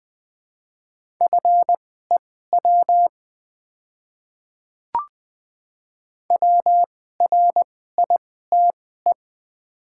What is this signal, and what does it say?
Morse code, heard as a keyed tone, about 20 wpm, spaced as for 14 wpm, 690 Hz. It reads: FEW WRITE
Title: 2x Character Spacing - Top 300 Words - 20wpm